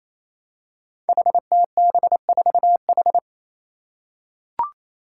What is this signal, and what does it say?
Morse code, heard as a keyed tone, about 28 wpm, 700 Hz. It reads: HTB4H
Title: Five Challenging Letter-Number Combinations 28wpm